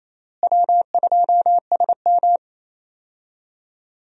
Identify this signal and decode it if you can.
Morse code, heard as a keyed tone, about 28 wpm, 700 Hz. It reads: W2SM
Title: USA Call Signs - 4 Characters - Repeated 3x - 28wpm